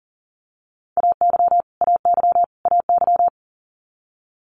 Morse code, heard as a keyed tone, about 40 wpm, 700 Hz. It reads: AY AY AY
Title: Multi-letter Phonemes Repeated 3x 40wpm